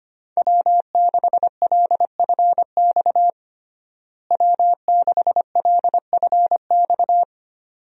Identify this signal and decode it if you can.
Morse code, heard as a keyed tone, about 25 wpm, 700 Hz. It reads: W6LFX W6LFX
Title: USA Call Signs - 5 Characters - Repeated 2x - 25wpm